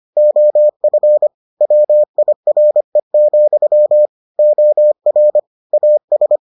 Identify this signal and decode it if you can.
Morse code, heard as a keyed tone, about 25 wpm, 600 Hz. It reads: OF WIRE, OR AS